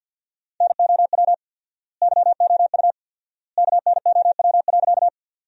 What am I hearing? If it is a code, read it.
Morse code, heard as a keyed tone, about 46 wpm, 700 Hz. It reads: NOW YOU KNOW.
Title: 4x Word-Spacing - Sentences from Top 100 Words - 50wpm